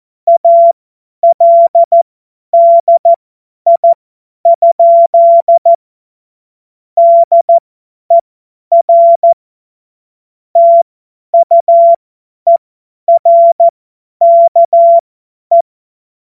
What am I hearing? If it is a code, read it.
Morse code, heard as a keyed tone, about 14 wpm, spaced as for 10 wpm, 680 Hz. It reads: ALDI? DER TUERKE
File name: Witz-30-September-2025-50-F70